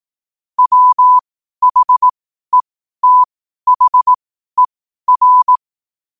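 Morse code, heard as a keyed tone, about 18 wpm, spaced as for 12 wpm, 1000 Hz. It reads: WHETHER
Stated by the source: Morse code sent at 12 words per minute.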